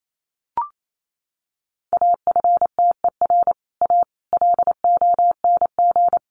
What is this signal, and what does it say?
Morse code, heard as a keyed tone, about 28 wpm, 700 Hz. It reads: AFTER A LONG
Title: A Sound Of Thunder 28wpm